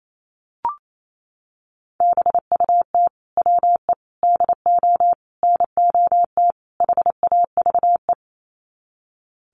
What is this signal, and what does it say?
Morse code, heard as a keyed tone, about 28 wpm, 700 Hz. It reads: BUT WE DO NOT HAVE
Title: Sets of 5 Words Encore2 28wpm